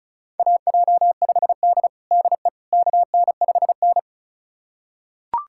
Morse code, heard as a keyed tone, about 36 wpm, 700 Hz. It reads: AJ5D DE KN5N
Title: USA Callsign DE Callsign - 4 Characters - 35wpm